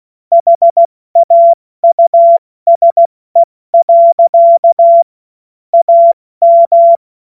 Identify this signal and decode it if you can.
Morse code, heard as a keyed tone, about 16 wpm, 680 Hz. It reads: HAUSE. AM
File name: Morsekurs / Witze / Witz 21. September 2021, 70 F80